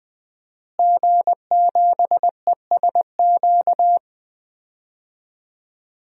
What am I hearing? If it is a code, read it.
Morse code, heard as a keyed tone, about 20 wpm, 700 Hz. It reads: G7ESQ
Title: DXCC-100 Call Signs - 5 Characters - Repeated 3x - 20wpm